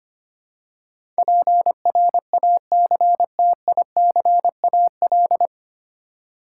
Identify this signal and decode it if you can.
Morse code, heard as a keyed tone, about 25 wpm, 700 Hz. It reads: PRACTICAL
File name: Nine Letter Words - 25wpm